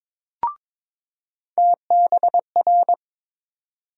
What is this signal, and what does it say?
Morse code, heard as a keyed tone, about 21 wpm, 700 Hz. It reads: TBR
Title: USA Call Sign Postfixes - 3 Letters 22wpm